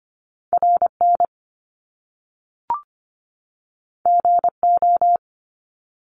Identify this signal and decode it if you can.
Morse code, heard as a keyed tone, about 25 wpm, 700 Hz. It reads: RN GO
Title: Top Two Letter Combinations in Words 25wpm